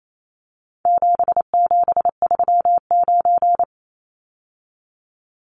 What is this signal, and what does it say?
Morse code, heard as a keyed tone, about 28 wpm, 700 Hz. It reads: 7739
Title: Numbers - 4-Digit 28wpm